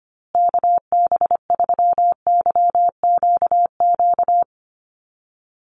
Morse code, heard as a keyed tone, about 25 wpm, 700 Hz. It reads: KB3YQQ
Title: USA Call Signs Repeated 3x 25wpm